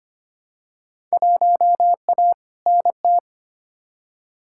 Morse code, heard as a keyed tone, about 25 wpm, 700 Hz. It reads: 1A NT
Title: Field Day Exchanges Encore - 25wpm